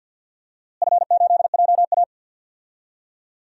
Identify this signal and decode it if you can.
Morse code, heard as a keyed tone, about 53 wpm, 700 Hz. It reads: F8JA
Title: DXCC-100 Call Signs - 4 Characters - Repeated 3x - 50wpm